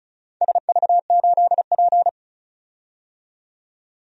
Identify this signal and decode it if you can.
Morse code, heard as a keyed tone, about 35 wpm, 700 Hz. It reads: SV8P
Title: DXCC-100 Call Signs - 4 Characters - 35wpm